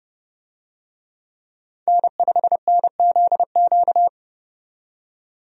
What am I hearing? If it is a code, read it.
Morse code, heard as a keyed tone, about 30 wpm, 700 Hz. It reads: N5NZQ
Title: USA Call Signs - 5 Characters - 30wpm